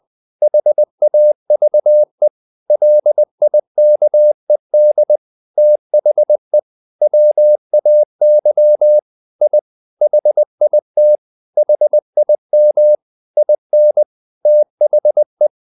Morse code, heard as a keyed tone, about 20 wpm, 600 Hz. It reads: HAVE LIKED THE WAY I HIT HIM IN THE
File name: the old man and the sea part 2 20wpm